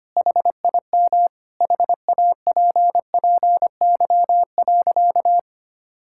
Morse code, heard as a keyed tone, about 25 wpm, 700 Hz. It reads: HIM HAPPY.
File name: practice / 25wpm / Sentences with 2-Word Phrases - 25wpm